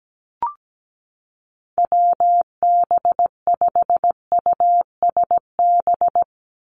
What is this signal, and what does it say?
Morse code, heard as a keyed tone, about 17 wpm, 700 Hz. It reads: WB5USB
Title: NA SOTA Chaser Callsigns 17wpm